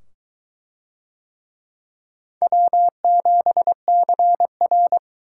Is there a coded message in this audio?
Yes: Morse code from 2.4 to 5.0 s, heard as a keyed tone, about 22 wpm, 700 Hz. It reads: W7CR